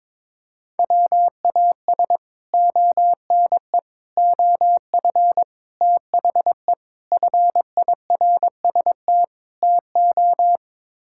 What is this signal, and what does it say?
Morse code, heard as a keyed tone, about 21 wpm, 700 Hz. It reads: WAS ONE OF THE FIRST TO
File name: practice / 22wpm / Sets of 6 Words - 22wpm